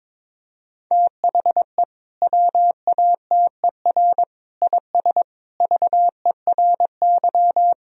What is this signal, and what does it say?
Morse code, heard as a keyed tone, about 22 wpm, 700 Hz. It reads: THE WATER IS VERY
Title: Sentences from Top 1000 Words - Review Part 3 - 22wpm